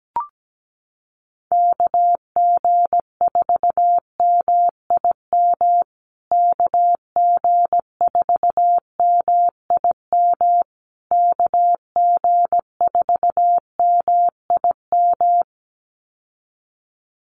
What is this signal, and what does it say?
Morse code, heard as a keyed tone, about 17 wpm, 700 Hz. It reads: KG4MIM KG4MIM KG4MIM
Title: USA Call Signs Repeated 3x 17wpm